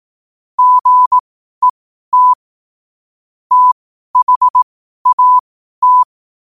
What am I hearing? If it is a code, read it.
Morse code, heard as a keyed tone, about 18 wpm, spaced as for 12 wpm, 1000 Hz. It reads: GET THAT
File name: qotdfiles / 12wpm